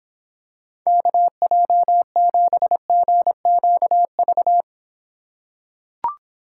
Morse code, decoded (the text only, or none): KJ7GQV